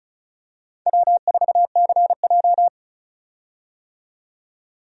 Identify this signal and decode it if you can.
Morse code, heard as a keyed tone, about 36 wpm, 690 Hz. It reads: W4CJ
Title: USA Call Signs - 4 Characters - 35wpm